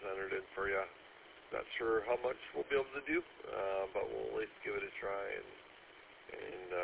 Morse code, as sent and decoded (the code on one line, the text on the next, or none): none